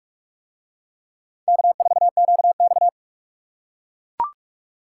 Morse code, heard as a keyed tone, about 41 wpm, 700 Hz. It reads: K4QX